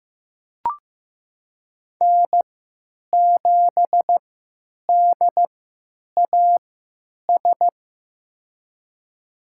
Morse code, heard as a keyed tone, about 15 wpm, spaced as for 9 wpm, 700 Hz. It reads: N7DAS
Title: USA Call Signs - 5 Characters - 3x Character Spacing - 15wpm